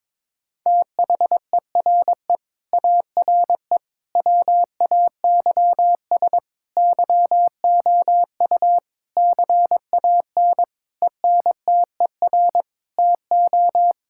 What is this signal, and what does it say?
Morse code, heard as a keyed tone, about 22 wpm, 700 Hz. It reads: THERE ARE WAYS YOU CAN ENTER TO
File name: practice / 22wpm / Sentences with 8-Word Phrases - 22wpm